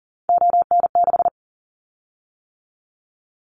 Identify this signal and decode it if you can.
Morse code, heard as a keyed tone, about 40 wpm, 700 Hz. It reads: ON6
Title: Callsign Prefixes for Top 100 DXCC - 40wpm